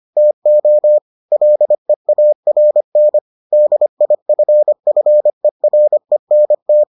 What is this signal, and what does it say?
Morse code, heard as a keyed tone, about 25 wpm, 600 Hz. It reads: TO LEARN DIFFERENT